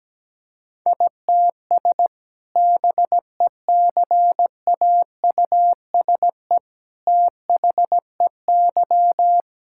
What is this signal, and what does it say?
Morse code, heard as a keyed tone, about 17 wpm, 700 Hz. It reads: ITS BECAUSE THEY
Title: Sentences from Top 1000 Words - Review Part 6 - 17wpm